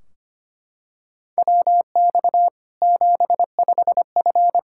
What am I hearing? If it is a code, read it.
Morse code, heard as a keyed tone, about 25 wpm, 700 Hz. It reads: WX 75F
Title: K1USN SST QSO Practice 25wpm